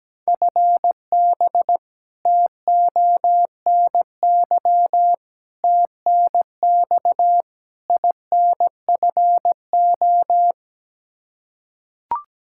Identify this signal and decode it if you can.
Morse code, heard as a keyed tone, about 17 wpm, 700 Hz. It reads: FB TONY TNX INFO
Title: Practice QSO - Exchange 3 - More Info - 17wpm